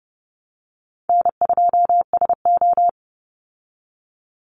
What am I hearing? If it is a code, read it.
Morse code, heard as a keyed tone, about 30 wpm, 700 Hz. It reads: N2SO
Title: Easy Call Signs Repeated 2x - 30wpm